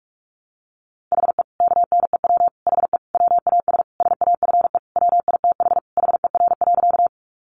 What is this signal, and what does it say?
Morse code, heard as a keyed tone, about 44 wpm, 700 Hz. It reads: HE KNEW HE WAS SAFE WITH HER.
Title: Sentences from Top 900 Words 45wpm